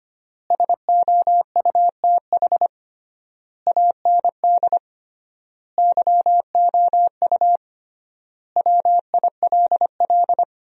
Morse code, heard as a keyed tone, about 25 wpm, 700 Hz. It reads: SOUTH AND YOU WILL
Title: 3x Word-Spacing - Sentences from Top 400 Words - 25wpm